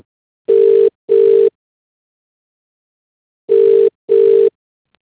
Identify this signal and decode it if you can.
Ringback tone (UK), 400 + 450 Hz.